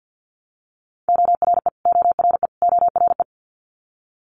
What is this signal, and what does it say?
Morse code, heard as a keyed tone, about 50 wpm, 700 Hz. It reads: ORE ORE ORE
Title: Multi-letter Phonemes Repeated 3x 50wpm